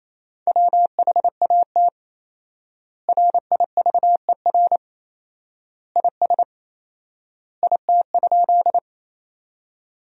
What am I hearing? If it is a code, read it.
Morse code, heard as a keyed tone, about 28 wpm, 700 Hz. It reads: WHAT RIVER IS IT?
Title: 4x Word-Spacing - Sentences from Top 300 Words - 28wpm